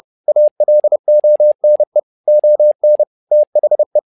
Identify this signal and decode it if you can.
Morse code, heard as a keyed tone, about 30 wpm, 600 Hz. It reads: ALONE ON THE
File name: learning center / tower trouble / tower trouble 30wpm